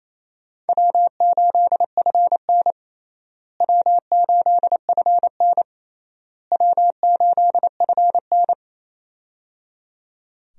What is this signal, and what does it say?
Morse code, heard as a keyed tone, about 28 wpm, 700 Hz. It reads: W8FN W8FN W8FN
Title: USA Call Signs - 4 Characters - Repeated 3x - 28wpm